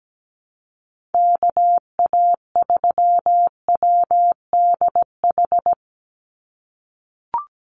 Morse code, heard as a keyed tone, about 17 wpm, 700 Hz. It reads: KA3WDH
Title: USA Call Signs 17wpm